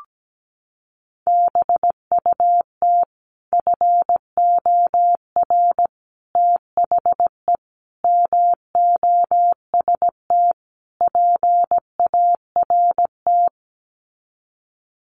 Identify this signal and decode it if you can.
Morse code, heard as a keyed tone, about 17 wpm, 700 Hz. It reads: BUT FOR THE MOST PART
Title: Sets of 5 Words Encore 17wpm